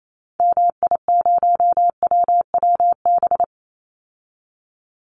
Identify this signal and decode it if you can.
Morse code, heard as a keyed tone, about 28 wpm, 690 Hz. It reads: MI0WWB
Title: Call Signs Repeated 3x - 28wpm